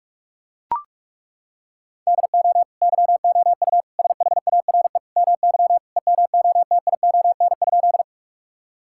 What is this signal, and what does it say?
Morse code, heard as a keyed tone, about 43 wpm, 700 Hz. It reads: DO YOU SHARE MY EMOTION?
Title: Sentences from Top 1600 Words 45wpm